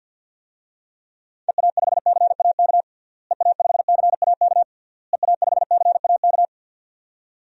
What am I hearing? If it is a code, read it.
Morse code, heard as a keyed tone, about 53 wpm, 700 Hz. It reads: EA5CAK EA5CAK EA5CAK
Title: DXCC-100 Call Signs - 6 Characters - Repeated 3x - 50wpm